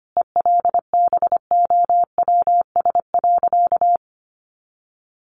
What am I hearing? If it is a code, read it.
Morse code, heard as a keyed tone, about 25 wpm, 700 Hz. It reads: ELBOWS.
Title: A Sound Of Thunder 25wpm